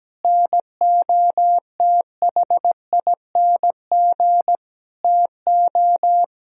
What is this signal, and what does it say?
Morse code, heard as a keyed tone, about 17 wpm, 700 Hz. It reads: NOTHING TO
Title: Sentences from Top 1000 Words - Review Part 1 - 17wpm